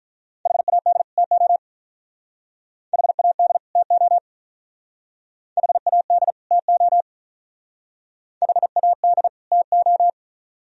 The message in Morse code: HAD TO HAD TO HAD TO HAD TO

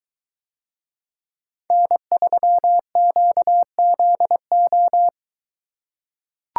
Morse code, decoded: N3QZO

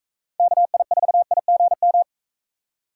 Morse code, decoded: KI4IGM

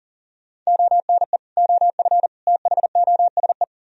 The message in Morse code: ONE OF THOSE

